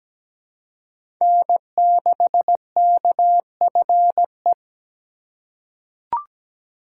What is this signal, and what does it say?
Morse code, heard as a keyed tone, about 17 wpm, 700 Hz. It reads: N6KFE
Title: USA Call Signs - 5 Characters - 3x Character Spacing - 17wpm